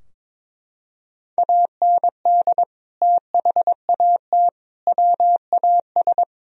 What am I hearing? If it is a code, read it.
Morse code, heard as a keyed tone, about 22 wpm, 700 Hz. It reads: AND THAT WAS
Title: Sentences from Top 1000 Words - Review Part 3 - 22wpm